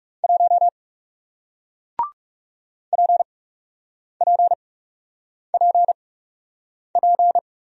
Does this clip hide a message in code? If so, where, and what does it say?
Yes: Morse code from 0.2 to 7.4 s, heard as a keyed tone, about 40 wpm, spaced as for 8 wpm, 700 Hz. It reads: 1 PPPP